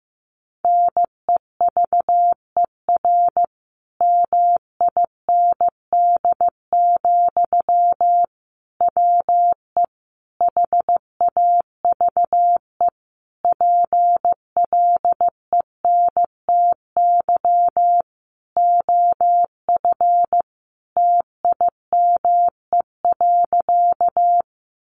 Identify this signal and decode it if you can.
Morse code, heard as a keyed tone, about 15 wpm, 700 Hz. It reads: NEVER MIND, WE HAVE PLENTY OF TIME.
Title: Sentences from Top 1300 Words 15wpm